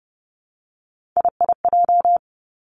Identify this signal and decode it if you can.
Morse code, heard as a keyed tone, about 30 wpm, 700 Hz. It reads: IIJ
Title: Three Challenging Letter-Number Combinations 30wpm